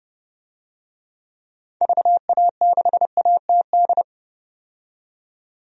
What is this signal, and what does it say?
Morse code, heard as a keyed tone, about 32 wpm, 700 Hz. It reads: VA6ATD
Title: DXCC-100 Call Signs - 6 Characters - 30wpm